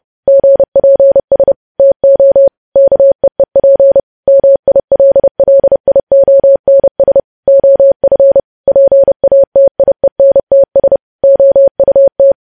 Morse code, heard as a keyed tone, about 30 wpm, 550 Hz. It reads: GPS TO KEEP MILLIONS OF PATIENTS OUT
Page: news headlines in morse code at 30 wpm